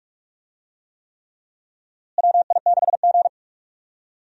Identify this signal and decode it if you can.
Morse code, heard as a keyed tone, about 41 wpm, 700 Hz. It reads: WI6G
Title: USA Call Signs - 4 Characters - Repeated 3x - 45wpm